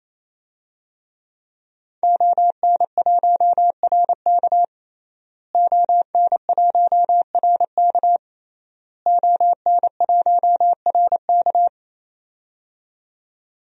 Morse code, heard as a keyed tone, about 28 wpm, 700 Hz. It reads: ON1RK ON1RK ON1RK
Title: DXCC-100 Call Signs - 5 Characters - Repeated 3x - 28wpm